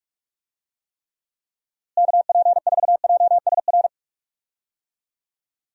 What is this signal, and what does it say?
Morse code, heard as a keyed tone, about 48 wpm, 700 Hz. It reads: KP4JSR
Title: USA Call Signs - 6 Characters - Repeated 2x - 45wpm